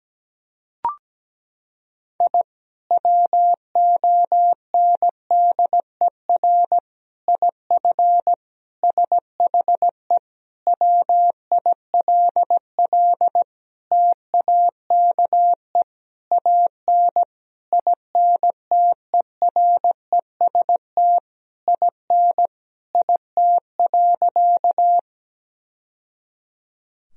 Morse code, heard as a keyed tone, about 17 wpm, 700 Hz. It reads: I WONDER IF SHE WILL TAKE AN INTEREST IN IT.
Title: Sentences from Top 1000 Words - Review Part 3 - 17wpm